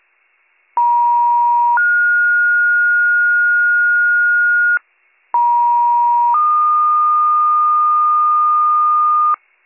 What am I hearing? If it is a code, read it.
Two-tone page, heard as a one-second tone, then a three-second tone, about 950 then 1430 Hz, about 950 then 1220 Hz, with noise. It is heard from 0.8 to 9.4 s.